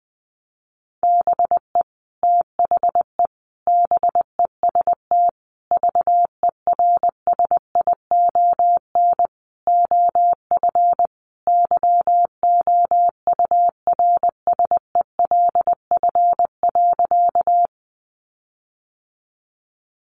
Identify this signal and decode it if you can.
Morse code, heard as a keyed tone, about 20 wpm, 700 Hz. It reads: BE THE BEST VERSION OF YOURSELF.
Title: Sentences from Top 1400 Words 20wpm